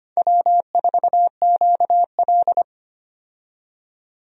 Morse code, heard as a keyed tone, about 25 wpm, 700 Hz. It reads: W4QL
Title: USA Call Signs - 4 Characters - 25wpm